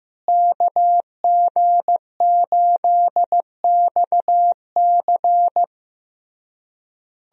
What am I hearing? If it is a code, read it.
Morse code, heard as a keyed tone, about 15 wpm, 700 Hz. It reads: KG8XC
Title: USA Call Signs - 5 Characters - Repeated 3x - 15wpm